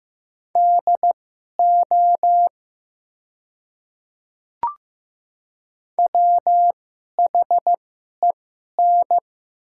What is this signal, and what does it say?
Morse code, heard as a keyed tone, about 15 wpm, spaced as for 11 wpm, 700 Hz. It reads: DO WHEN
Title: 2x Character Spacing - Top 300 Words - 15wpm